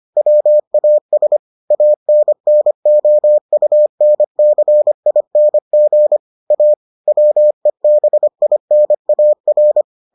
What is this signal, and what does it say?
Morse code, heard as a keyed tone, about 25 wpm, 600 Hz. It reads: WAS ANNOUNCING A WEBINAR